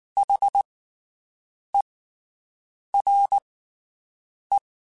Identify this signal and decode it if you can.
Morse code, heard as a keyed tone, about 19 wpm, spaced as for 7 wpm, 800 Hz. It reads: HERE
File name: (spaced) / QSOs Spaced-mp3s / WD0BGP 1 @18 sp